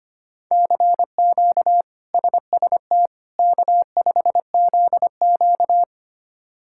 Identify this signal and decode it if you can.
Morse code, heard as a keyed tone, about 25 wpm, 700 Hz. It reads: CQ SST K5ZQ
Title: K1USN SST QSO Practice - 25wpm